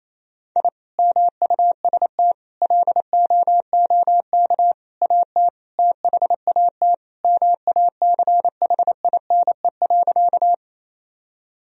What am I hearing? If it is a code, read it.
Morse code, heard as a keyed tone, about 28 wpm, 700 Hz. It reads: I MUST LOOK AT THAT MACHINE.
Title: Sentences from Top 1000 Words - Review Part 2 - 28wpm